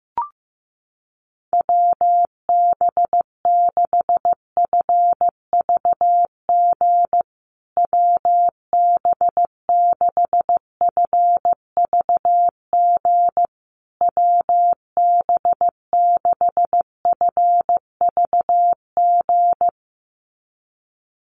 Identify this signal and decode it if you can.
Morse code, heard as a keyed tone, about 14 wpm, 700 Hz. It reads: WB6FVG WB6FVG WB6FVG
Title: USA Call Signs Repeated 3x 15wpm